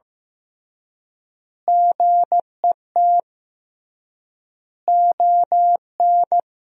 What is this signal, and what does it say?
Morse code, heard as a keyed tone, about 15 wpm, 700 Hz. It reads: GET ON
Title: 3x Word-Spacing - Sentences from Top 200 Words - 15wpm